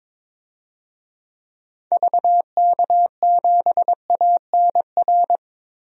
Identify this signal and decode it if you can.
Morse code, heard as a keyed tone, about 22 wpm, 700 Hz. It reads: VK7ANR
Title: DXCC-100 Call Signs - 6 Characters - Repeated 2x - 22wpm